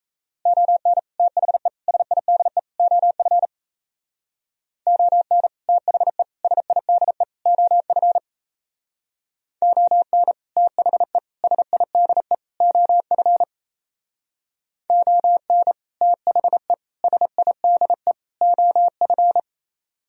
Morse code, heard as a keyed tone, about 37 wpm, 700 Hz. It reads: ON THE SIDE OF ON THE SIDE OF ON THE SIDE OF ON THE SIDE OF